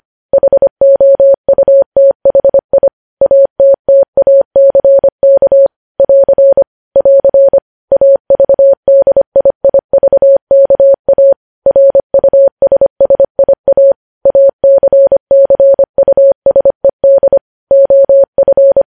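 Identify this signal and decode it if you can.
Morse code, heard as a keyed tone, about 25 wpm, 550 Hz. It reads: HOUTHI ATTACK <AR> <AR> AVDIIVKA RUSSIA ACCUSED OF
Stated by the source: Morse code news headlines sent at 25 wpm.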